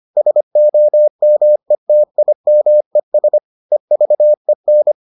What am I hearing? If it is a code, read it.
Morse code, heard as a keyed tone, about 25 wpm, 600 Hz. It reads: SOMETIMES EVEN